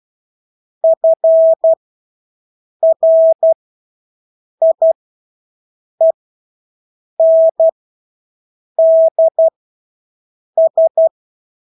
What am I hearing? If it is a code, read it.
Morse code, heard as a keyed tone, about 12 wpm, spaced as for 6 wpm, 650 Hz. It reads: FRIENDS
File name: Engelske ord 12-6 wpm